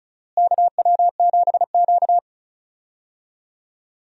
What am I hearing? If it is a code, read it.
Morse code, heard as a keyed tone, about 36 wpm, 700 Hz. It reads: KW7Q
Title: USA Call Signs - 4 Characters - Repeated 2x - 35wpm